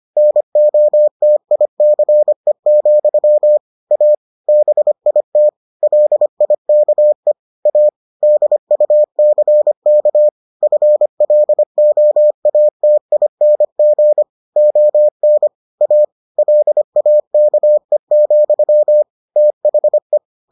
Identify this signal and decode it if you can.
Morse code, heard as a keyed tone, about 25 wpm, 600 Hz. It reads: NOTICE, A BIT LIKE A DUCK FLOATING ON A LAKE, THE